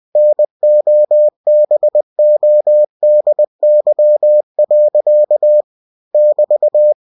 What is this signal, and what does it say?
Morse code, heard as a keyed tone, about 20 wpm, 600 Hz. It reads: NOBODY. <BT>
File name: cw-books / winnie-the-pooh / winnie-20wpm-page-0014